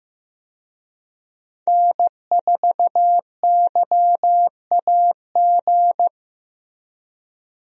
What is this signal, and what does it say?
Morse code, heard as a keyed tone, about 15 wpm, 700 Hz. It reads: N4YAG
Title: USA Call Signs - 5 Characters - Repeated 3x - 15wpm